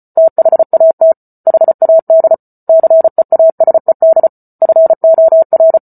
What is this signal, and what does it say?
Morse code, heard as a keyed tone, about 34 wpm, 650 Hz. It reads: THAT HAD CEASED FOR
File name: Book Two / 08-Chapter 8 - 34 WPM